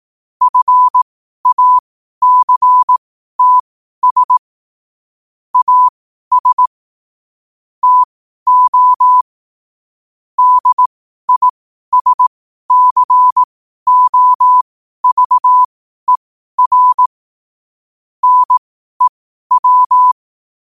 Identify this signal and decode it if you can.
Morse code, heard as a keyed tone, about 18 wpm, spaced as for 12 wpm, 1000 Hz. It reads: FACTS AS TO DISCOVER NEW
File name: qotdfiles / 12wpm